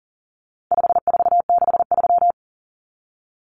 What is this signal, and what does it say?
Morse code, heard as a keyed tone, about 40 wpm, 700 Hz. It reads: 5463
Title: Numbers - 4-Digit 40wpm